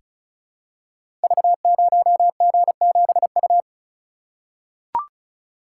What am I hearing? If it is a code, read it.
Morse code, heard as a keyed tone, about 36 wpm, 700 Hz. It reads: V0G7U
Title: Code Groups - 5 Letter:Numbers - 35wpm